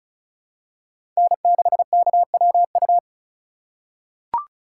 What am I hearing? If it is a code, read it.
Morse code, heard as a keyed tone, about 34 wpm, 700 Hz. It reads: N6KWU
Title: USA Call Signs - 5 Characters - 3x Character Spacing - 35wpm